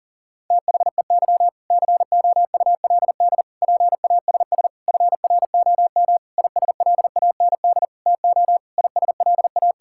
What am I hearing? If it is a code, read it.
Morse code, heard as a keyed tone, about 40 wpm, 700 Hz. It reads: THEY COULD PASS FROM ISLAND TO ISLA
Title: Sentences from Top 1000 Words - Review Part 6 - 40wpm